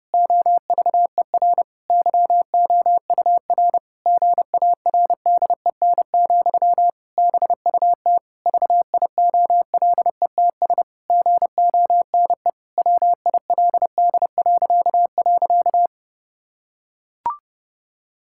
Morse code, heard as a keyed tone, about 30 wpm, 700 Hz. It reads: OVER YOUR GARDEN, BUT VIOLETS GONE WILD..
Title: Pangram Challenge v2 - 30wpm